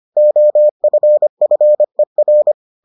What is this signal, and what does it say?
Morse code, heard as a keyed tone, about 25 wpm, 600 Hz. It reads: OFFER